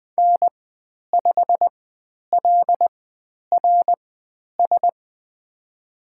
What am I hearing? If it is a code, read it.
Morse code, heard as a keyed tone, about 20 wpm, spaced as for 10 wpm, 700 Hz. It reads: N5LRS
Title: 095 Lesson 16g - Letter c - Call Signs 20wpm